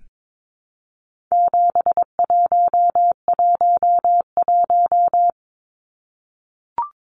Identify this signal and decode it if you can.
Morse code, heard as a keyed tone, about 22 wpm, 700 Hz. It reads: 7111
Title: Numbers - 4-Digit 22wpm